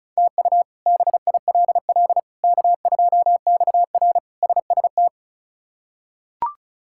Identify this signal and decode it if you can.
Morse code, heard as a keyed tone, about 34 wpm, 700 Hz. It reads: TU BILL K2XR SST
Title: K1USN SST QSO Practice - 35wpm